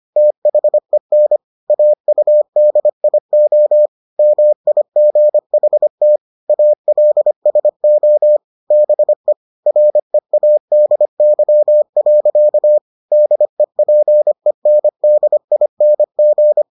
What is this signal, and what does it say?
Morse code, heard as a keyed tone, about 25 wpm, 600 Hz. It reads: THEN AUDIO MIGHT ALSO BE READY. DEPENDING